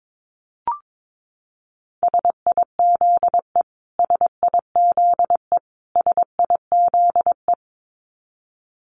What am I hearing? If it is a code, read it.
Morse code, heard as a keyed tone, about 22 wpm, 700 Hz. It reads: SIZE SIZE SIZE
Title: Top 500 Words Repeated 22wpm